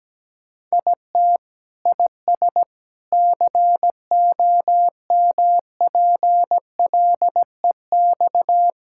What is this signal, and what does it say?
Morse code, heard as a keyed tone, about 17 wpm, 700 Hz. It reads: IT IS COMPLEX
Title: Sentences from Top 1000 Words - Review Part 6 - 17wpm